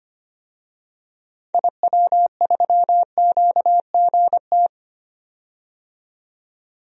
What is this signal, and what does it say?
Morse code, heard as a keyed tone, about 25 wpm, 700 Hz. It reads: IW3QGT
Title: DXCC-100 Call Signs - 6 Characters - 25wpm